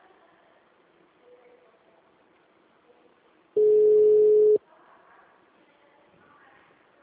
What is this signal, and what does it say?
Ringback tone (Europe, 425 Hz).
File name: out- -2015- - - .1911